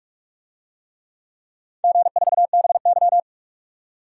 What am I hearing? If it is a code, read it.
Morse code, heard as a keyed tone, about 46 wpm, 700 Hz. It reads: G4BY
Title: DXCC-100 Call Signs - 4 Characters - 45wpm